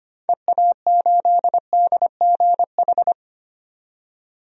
Morse code, heard as a keyed tone, about 25 wpm, 700 Hz. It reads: EA8DGH
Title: DXCC-100 Call Signs - 6 Characters - 25wpm